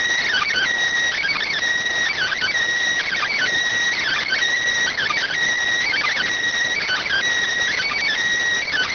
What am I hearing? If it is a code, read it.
Emergency Alert System (SAME) tones (header bursts), heard as harsh buzzing bursts. It is heard from 1.2 to 8.4 s.